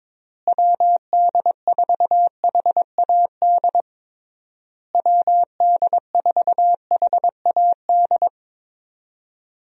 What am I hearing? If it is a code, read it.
Morse code, heard as a keyed tone, about 22 wpm, 700 Hz. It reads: WD4HAD WD4HAD
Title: USA Call Signs - 6 Characters - Repeated 3x - 22wpm